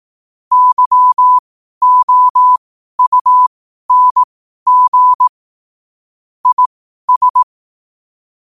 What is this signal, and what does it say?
Morse code, heard as a keyed tone, about 18 wpm, spaced as for 12 wpm, 1000 Hz. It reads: YOUNG IS
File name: qotdfiles / 12wpm